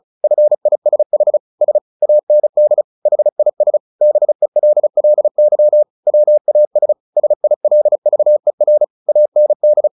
Morse code, heard as a keyed tone, about 34 wpm, 600 Hz. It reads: FISH S AND HIS BELLY WAS SILVER AND
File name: the old man and the sea 2 35wpm